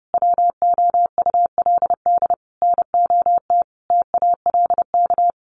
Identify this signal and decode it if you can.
Morse code, heard as a keyed tone, about 30 wpm, 700 Hz. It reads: WOULD NOT TALK
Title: Sentences from Top 800 Words 30wpm